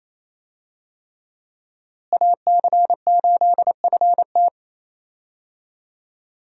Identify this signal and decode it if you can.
Morse code, heard as a keyed tone, about 28 wpm, 700 Hz. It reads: AC8FT
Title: USA Call Signs - 5 Characters - Repeated 3x - 28wpm